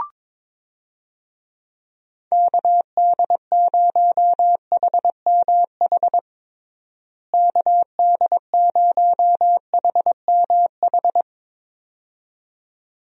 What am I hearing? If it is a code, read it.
Morse code, heard as a keyed tone, about 22 wpm, 700 Hz. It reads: KD0HMH KD0HMH
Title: USA Call Signs - 6 Characters - Repeated 2x - 22wpm